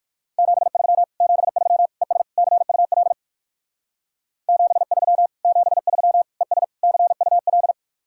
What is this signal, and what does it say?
Morse code, heard as a keyed tone, about 64 wpm, spaced as for 56 wpm, 700 Hz. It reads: 73 73 ES CUL 73 73 ES CUL
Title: Speed-Racing Common QSO Phrases Encore - 45wpm